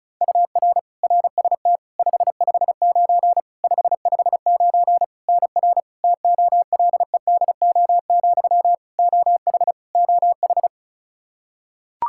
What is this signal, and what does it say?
Morse code, heard as a keyed tone, about 36 wpm, 700 Hz. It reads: UR RST 559 559 NR TOLEDO, OH OH
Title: Practice QSO - Exchange 1 - Beginning - 35wpm